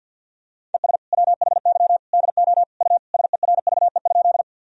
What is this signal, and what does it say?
Morse code, heard as a keyed tone, about 52 wpm, 700 Hz. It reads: ES WHY DO U SERVE?
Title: Sentences from Top 400 Words and Abbreviations 50wpm